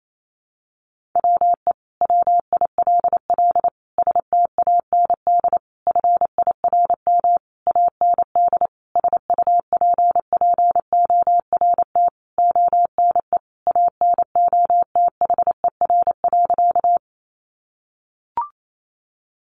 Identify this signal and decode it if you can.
Morse code, heard as a keyed tone, about 28 wpm, 700 Hz. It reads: WE WILL STAND FIRM AND SUPPORT ONE ANOTHER.
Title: Sentences from Top 1400 Words 28wpm